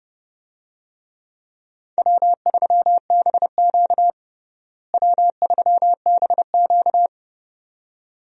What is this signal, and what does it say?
Morse code, heard as a keyed tone, about 30 wpm, 700 Hz. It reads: W3BQ W3BQ
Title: USA Call Signs - 4 Characters - Repeated 2x - 30wpm